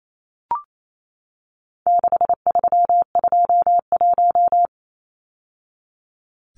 Morse code, heard as a keyed tone, about 28 wpm, 700 Hz. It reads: 6321
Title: Numbers - 4-Digit 28wpm